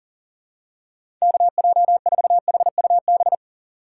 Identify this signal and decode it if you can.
Morse code, heard as a keyed tone, about 40 wpm, 700 Hz. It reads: KJ4HUB
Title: USA Call Signs - 6 Characters - 3x Character Spacing - 40wpm